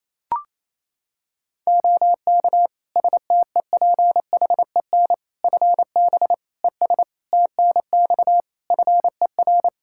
Morse code, heard as a keyed tone, about 27 wpm, 700 Hz. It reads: OK STEPHEN FB ES TNX FER
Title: K1USN SST QSO Practice 28wpm